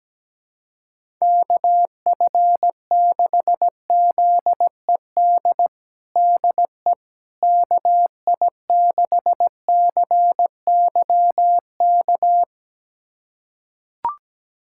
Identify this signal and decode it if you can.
Morse code, heard as a keyed tone, about 17 wpm, 700 Hz. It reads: KF6ZED DE KI6CYK
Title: USA Callsign DE Callsign - 6 Characters - 17wpm